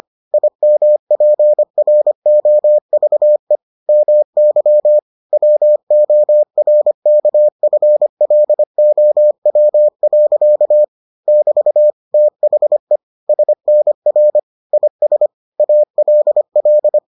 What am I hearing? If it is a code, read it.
Morse code, heard as a keyed tone, about 25 wpm, 600 Hz. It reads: IMPROVE MY WORKFLOW. <BT> THE SNR IS ALL